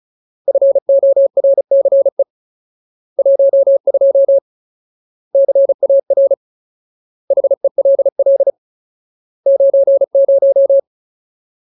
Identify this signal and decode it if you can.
Morse code, heard as a keyed tone, about 34 wpm, 550 Hz. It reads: FORCE 12 CAR HELL 90
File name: Ultimate Word List / 35WPM Ultimate